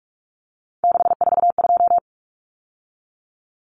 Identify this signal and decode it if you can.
Morse code, heard as a keyed tone, about 44 wpm, 700 Hz. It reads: 642
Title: Numbers - 3-Digit 45wpm